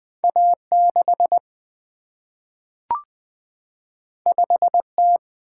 Morse code, heard as a keyed tone, about 20 wpm, 700 Hz. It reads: A 6 5 T
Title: Single-Letter Single-Number 20wpm